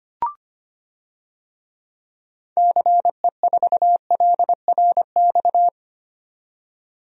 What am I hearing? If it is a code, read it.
Morse code, heard as a keyed tone, about 25 wpm, 700 Hz. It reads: CE4LRX
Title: DXCC-100 Call Signs - 6 Characters - 25wpm